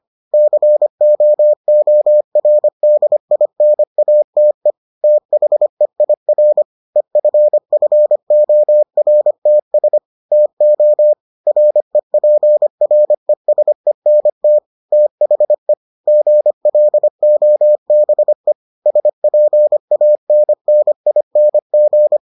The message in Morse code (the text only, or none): COORDINATE THEIR EFFORTS TO REPRESENT THE GLOBE SPANNING